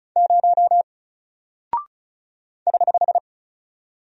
Morse code, heard as a keyed tone, about 36 wpm, 700 Hz. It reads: 0 <HH>
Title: Single Character (Letter:Number:Punctuation:Prosign) 35wpm